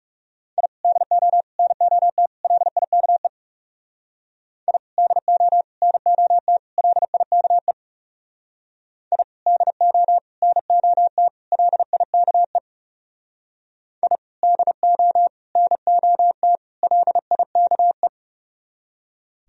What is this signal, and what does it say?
Morse code, heard as a keyed tone, about 36 wpm, 700 Hz. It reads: I DO NOT LIKE I DO NOT LIKE I DO NOT LIKE I DO NOT LIKE
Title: Speed-Racing - Sets of 4 Words 30wpm